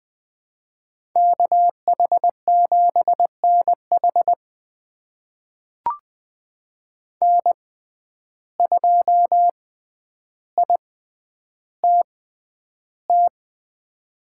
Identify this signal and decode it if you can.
Morse code, heard as a keyed tone, about 20 wpm, spaced as for 10 wpm, 700 Hz. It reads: KH7NH N2ITT
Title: USA Call Signs - 5 Characters - 6x Character Spacing - 20wpm